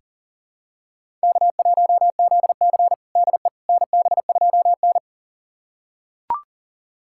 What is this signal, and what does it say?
Morse code, heard as a keyed tone, about 40 wpm, 700 Hz. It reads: K1ZC DE NB2N
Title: USA Callsign DE Callsign - 4 Characters - 40wpm